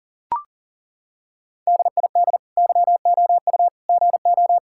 Morse code, heard as a keyed tone, about 40 wpm, 700 Hz. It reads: DID YOU GO
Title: Sentences from Top 1000 Words - Review Part 5 - 40wpm